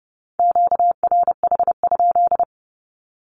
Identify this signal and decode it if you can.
Morse code, heard as a keyed tone, about 30 wpm, 700 Hz. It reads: QRH?
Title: Q-Codes 30wpm